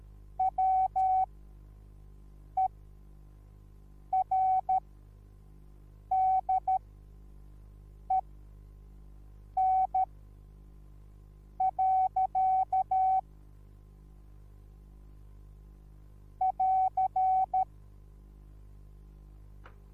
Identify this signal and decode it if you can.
Morse code, heard as a keyed tone, about 13 wpm, spaced as for 5 wpm, 760 Hz. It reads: WERDEN. <AR>
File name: Onlinelehrgang / morsen / DJ4UF Morsen21